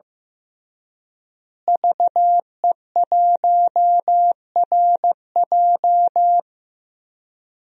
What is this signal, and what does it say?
Morse code, heard as a keyed tone, about 15 wpm, 700 Hz. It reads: VE1RJ
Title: DXCC-100 Call Signs - 5 Characters - Repeated 3x - 15wpm